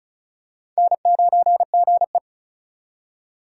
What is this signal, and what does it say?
Morse code, heard as a keyed tone, about 36 wpm, 700 Hz. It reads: N9GE
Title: USA Call Signs - 4 Characters - 2x Character Spacing - 35wpm